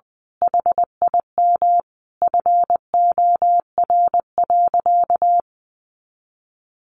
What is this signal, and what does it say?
Morse code, heard as a keyed tone, about 20 wpm, 700 Hz. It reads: HIM FOR.
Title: Sentences from Top 1000 Words 20wpm